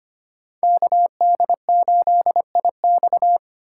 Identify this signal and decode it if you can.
Morse code, heard as a keyed tone, about 25 wpm, 700 Hz. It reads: KD8IX
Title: USA Call Signs - 5 Characters - 2x Character Spacing - 25wpm